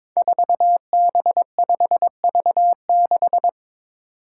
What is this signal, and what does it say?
Morse code, heard as a keyed tone, about 22 wpm, 700 Hz. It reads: 4B5V6
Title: Five Challenging Letter-Number Combinations 22wpm